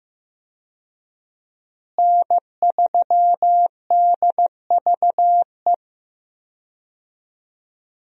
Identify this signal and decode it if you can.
Morse code, heard as a keyed tone, about 15 wpm, 700 Hz. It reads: N3DVE
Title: USA Call Signs - 5 Characters - Repeated 3x - 15wpm